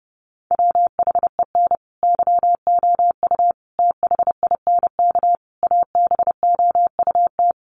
Morse code, heard as a keyed tone, about 30 wpm, 700 Hz. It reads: WHEN YOU THINK ABOUT
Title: Sets of 5 Words Encore2 30wpm